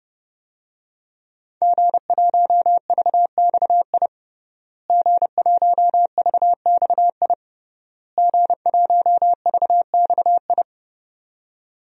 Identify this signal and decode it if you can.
Morse code, heard as a keyed tone, about 31 wpm, 700 Hz. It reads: G1VXI G1VXI G1VXI
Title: DXCC-100 Call Signs - 5 Characters - Repeated 3x - 30wpm